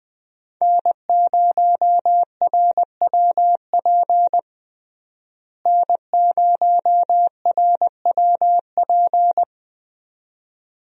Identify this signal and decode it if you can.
Morse code, heard as a keyed tone, about 20 wpm, 700 Hz. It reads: N0RWP N0RWP
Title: USA Call Signs - 5 Characters - Repeated 2x - 20wpm